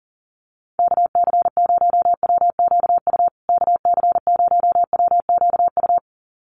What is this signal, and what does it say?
Morse code, heard as a keyed tone, about 40 wpm, 700 Hz. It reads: KC0WQU KC0WQU
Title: USA Call Signs Repeated 2x 40wpm